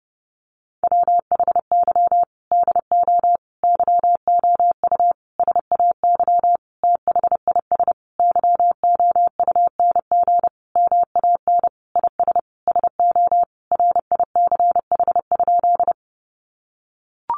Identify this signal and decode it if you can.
Morse code, heard as a keyed tone, about 30 wpm, 700 Hz. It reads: WHY DO YOU SAY THIS YOUNG MAN IS SO RICH?
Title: Sentences from Top 1100 Words 30wpm